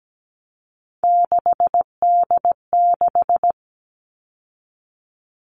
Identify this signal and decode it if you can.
Morse code, heard as a keyed tone, about 17 wpm, 700 Hz. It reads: 6D6
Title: All Cut Numbers - 3-Digit 17wpm